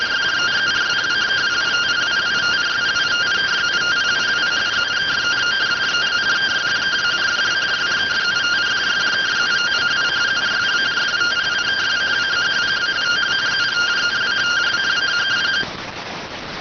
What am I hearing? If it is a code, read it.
Two-tone teleprinter signal, about 170 Hz shift.